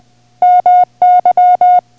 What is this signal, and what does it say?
Morse code, heard as a keyed tone, about 20 wpm, 720 Hz. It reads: MY